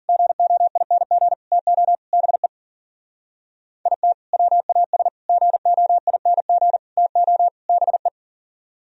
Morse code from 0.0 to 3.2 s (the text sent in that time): GOING TO BE